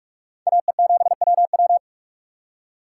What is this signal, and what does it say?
Morse code, heard as a keyed tone, about 58 wpm, 700 Hz. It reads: AE7WW